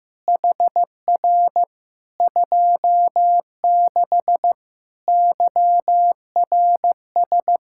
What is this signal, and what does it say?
Morse code, heard as a keyed tone, about 15 wpm, 700 Hz. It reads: HR 26 YRS
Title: Practice QSO - Exchange 3 - More Info - 15wpm